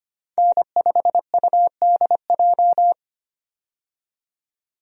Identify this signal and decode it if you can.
Morse code, heard as a keyed tone, about 25 wpm, 700 Hz. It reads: N5UDJ
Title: USA Call Signs - 5 Characters - 25wpm